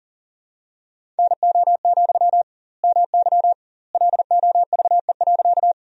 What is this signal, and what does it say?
Morse code, heard as a keyed tone, about 37 wpm, 700 Hz. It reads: NO, MY LOVE.
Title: 2x Word-Spacing - Sentences from Top 400 Words - 40wpm